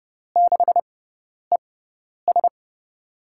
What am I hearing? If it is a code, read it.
Morse code, heard as a keyed tone, about 30 wpm, spaced as for 10 wpm, 700 Hz. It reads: 6ES
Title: USA Call Signs - 5 Characters - 6x Character Spacing - 30wpm